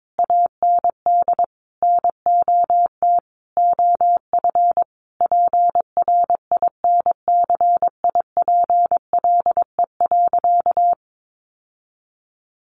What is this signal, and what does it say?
Morse code, heard as a keyed tone, about 22 wpm, 700 Hz. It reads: AND NOT OF PRINCIPLE.
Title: Sentences from Top 1400 Words 22wpm